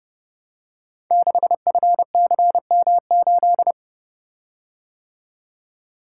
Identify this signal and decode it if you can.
Morse code, heard as a keyed tone, about 30 wpm, 700 Hz. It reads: 6FCM8
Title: Code Groups - 5 Letter:Numbers - 30wpm